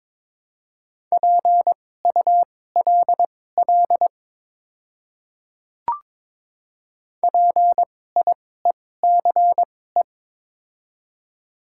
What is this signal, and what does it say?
Morse code, heard as a keyed tone, about 23 wpm, spaced as for 16 wpm, 700 Hz. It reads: PULL PIECE
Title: 2x Character Spacing - Top 500 Words - 22wpm